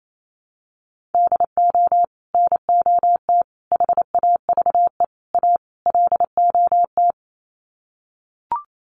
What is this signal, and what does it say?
Morse code, heard as a keyed tone, about 28 wpm, 700 Hz. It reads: DO NOT HAVE A LOT
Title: Sets of 5 Words Encore 28wpm